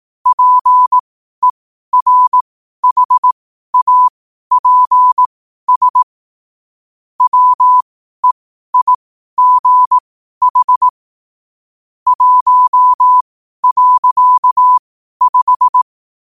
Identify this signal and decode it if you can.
Morse code, heard as a keyed tone, about 18 wpm, spaced as for 12 wpm, 1000 Hz. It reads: PERHAPS WEIGH 1.5
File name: qotdfiles / 12wpm